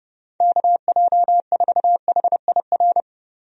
Morse code, heard as a keyed tone, about 30 wpm, 700 Hz. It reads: KJ4HIR
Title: USA Call Signs - 6 Characters - Repeated 3x - 30wpm